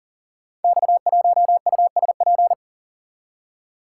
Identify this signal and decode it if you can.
Morse code, heard as a keyed tone, about 42 wpm, 690 Hz. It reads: X1USP
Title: Code Groups - 5 Letter:Numbers - 40wpm